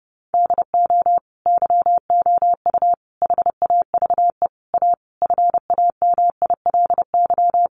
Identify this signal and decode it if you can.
Morse code, heard as a keyed tone, about 30 wpm, 700 Hz. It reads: DO YOU HAVE A FAMILY
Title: Sentences from Top 1500 Words 30wpm